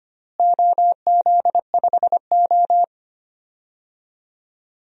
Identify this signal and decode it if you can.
Morse code, heard as a keyed tone, about 25 wpm, 700 Hz. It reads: OZ5O
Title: DXCC-100 Call Signs - 4 Characters - Repeated 2x - 25wpm